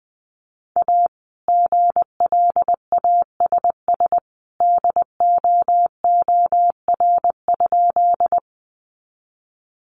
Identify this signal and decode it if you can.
Morse code, heard as a keyed tone, about 20 wpm, 700 Hz. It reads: A GLASS DOOR?
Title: Sentences from Top 1300 Words 20wpm